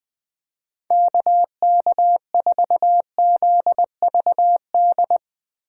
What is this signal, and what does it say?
Morse code, heard as a keyed tone, about 22 wpm, 700 Hz. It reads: KK4ZVD
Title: Speed Racing - USA 2x3 Call Signs 20wpm